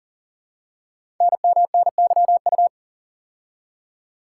Morse code, heard as a keyed tone, about 38 wpm, 690 Hz. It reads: NMNYU
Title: Code Groups - 5 Letters - 40wpm